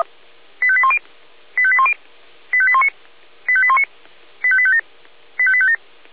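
Five-tone selective call (ZVEI), heard as quick runs of five tones, tones of about 70 ms. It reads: 85810 85810 85819 85819 85857 85857
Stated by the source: ZVEI five-tone selective call 85850 received by SDR.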